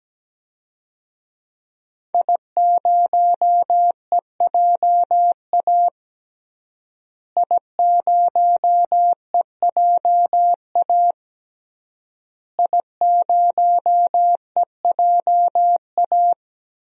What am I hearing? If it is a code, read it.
Morse code, heard as a keyed tone, about 17 wpm, 700 Hz. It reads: I0EJA I0EJA I0EJA
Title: DXCC-100 Call Signs - 5 Characters - Repeated 3x - 17wpm